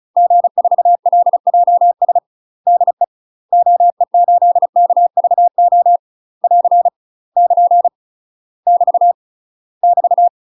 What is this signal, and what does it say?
Morse code, heard as a keyed tone, about 36 wpm, 700 Hz. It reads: G4LJS DE OE8KVO <AR> <KN> <BT> <BT>
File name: UK MASTER QSOs TRAINING / 35WPM / UK MASTER QSOs 0023